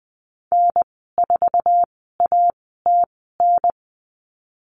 Morse code, heard as a keyed tone, about 20 wpm, spaced as for 14 wpm, 700 Hz. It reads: N4ATN
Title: 069 Lesson 12g - Letter d - Call Signs 20wpm